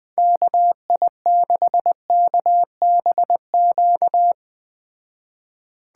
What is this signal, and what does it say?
Morse code, heard as a keyed tone, about 21 wpm, 700 Hz. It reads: KI6KBQ